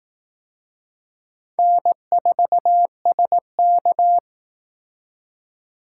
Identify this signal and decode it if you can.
Morse code, heard as a keyed tone, about 18 wpm, 700 Hz. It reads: N4SK
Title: Speed Racing - USA 1x2 Call Signs 15wpm